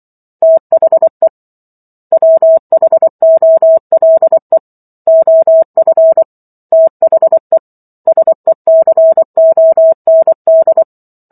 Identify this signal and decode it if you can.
Morse code, heard as a keyed tone, about 24 wpm, 650 Hz. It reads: THE WHOLE OF THE SECOND